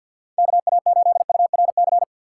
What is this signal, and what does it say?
Morse code, heard as a keyed tone, about 51 wpm, 690 Hz. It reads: KA8URC
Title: USA Call Signs - 6 Characters - 3x Character Spacing - 50wpm